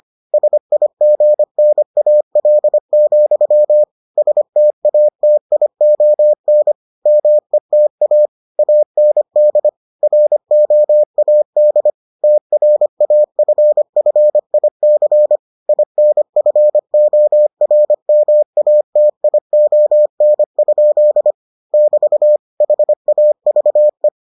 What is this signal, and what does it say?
Morse code, heard as a keyed tone, about 25 wpm, 600 Hz. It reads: SIGNAL, STATION META AND ROAD TRAFFIC INFORMATION? <BT> HAVE